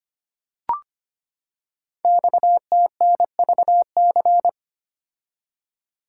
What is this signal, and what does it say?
Morse code, heard as a keyed tone, about 25 wpm, 700 Hz. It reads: XTNVC
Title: Code Groups - 5 Letter:Numbers - 25wpm